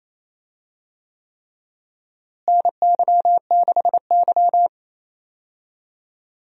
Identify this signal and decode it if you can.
Morse code, heard as a keyed tone, about 28 wpm, 700 Hz. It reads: NY6Y
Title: USA Call Signs - 4 Characters - Repeated 3x - 28wpm